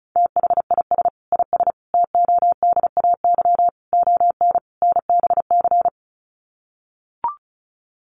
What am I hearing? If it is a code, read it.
Morse code, heard as a keyed tone, about 34 wpm, 700 Hz. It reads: THIS IS TODAY ON NBC
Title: Sets of 5 Words Encore 35wpm